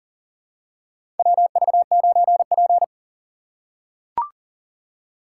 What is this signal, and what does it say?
Morse code, heard as a keyed tone, about 40 wpm, 700 Hz. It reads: WV9P
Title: USA Call Signs - 4 Characters - 4x Character Spacing - 40wpm